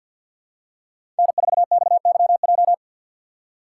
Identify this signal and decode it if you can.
Morse code, heard as a keyed tone, about 52 wpm, 700 Hz. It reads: N4XYJ
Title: USA Call Signs - 5 Characters - 2x Character Spacing - 50wpm